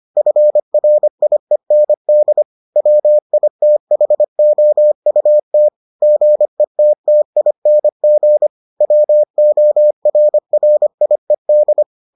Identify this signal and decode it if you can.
Morse code, heard as a keyed tone, about 25 wpm, 600 Hz. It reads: FRIEND WITHOUT GETTING WORRIED